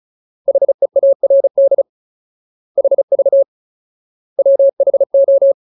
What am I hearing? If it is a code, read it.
Morse code, heard as a keyed tone, about 34 wpm, 550 Hz. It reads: HEARD HV WHO
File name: Ultimate Word List / 35WPM Ultimate